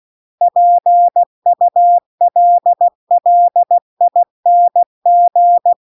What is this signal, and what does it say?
Morse code, heard as a keyed tone, about 16 wpm, 700 Hz. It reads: PULLING
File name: CW-The-Big-Twoheaded-River-2-16WPM